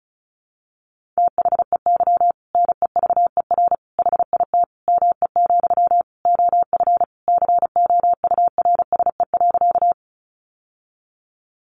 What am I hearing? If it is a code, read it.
Morse code, heard as a keyed tone, about 34 wpm, 700 Hz. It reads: THEY NEVER HIT ME, OF COURSE.
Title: Sentences from Top 1200 Words 35wpm